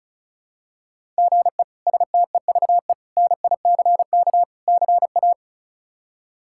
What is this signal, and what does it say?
Morse code, heard as a keyed tone, about 36 wpm, 700 Hz. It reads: GE STEVE NICK CA
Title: K1USN SST QSO Practice - 35wpm